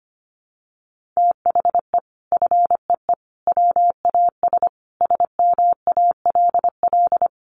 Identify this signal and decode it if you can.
Morse code, heard as a keyed tone, about 25 wpm, 700 Hz. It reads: THE FEE WAS SMALL
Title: Sentences from Top 1500 Words 25wpm